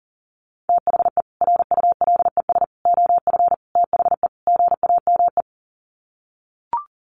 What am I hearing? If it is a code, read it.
Morse code, heard as a keyed tone, about 40 wpm, 700 Hz. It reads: THE RULES OF THE GAME
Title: Sets of 5 Words Encore2 40wpm